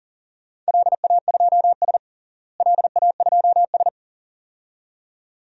Morse code, heard as a keyed tone, about 41 wpm, 700 Hz. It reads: LA2S LA2S
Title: DXCC-100 Call Signs - 4 Characters - Repeated 2x - 40wpm